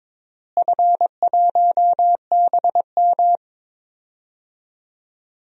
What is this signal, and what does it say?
Morse code, heard as a keyed tone, about 22 wpm, 700 Hz. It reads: F1BM
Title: DXCC-100 Call Signs - 4 Characters - Repeated 3x - 22wpm